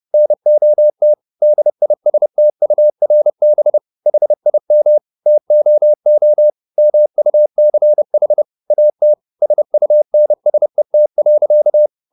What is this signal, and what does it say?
Morse code, heard as a keyed tone, about 30 wpm, 600 Hz. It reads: NOT DISTURB HIM TOO MUCH AT SUNSET.